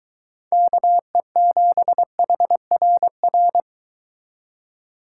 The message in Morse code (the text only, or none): KE7HRR